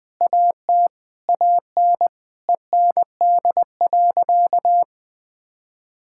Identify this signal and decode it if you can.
Morse code, heard as a keyed tone, about 20 wpm, 700 Hz. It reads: AT AN END.
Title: Sentences from Top 1600 Words 20wpm